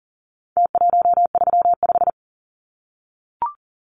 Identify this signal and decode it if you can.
Morse code, heard as a keyed tone, about 40 wpm, 700 Hz. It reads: T135
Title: Numbers - Contest Style - 4-Digit 40wpm